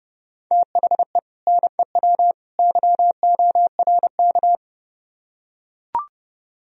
Morse code, heard as a keyed tone, about 30 wpm, 700 Hz. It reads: THE NEW YORK